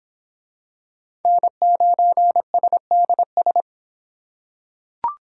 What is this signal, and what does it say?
Morse code, heard as a keyed tone, about 26 wpm, 700 Hz. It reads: N9SDS